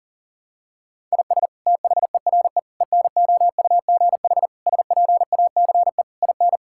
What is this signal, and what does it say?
Morse code, heard as a keyed tone, about 40 wpm, 700 Hz. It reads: IS THERE ENOUGH SPACE IN
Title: Sentences from Top 1000 Words - Review Part 5 - 40wpm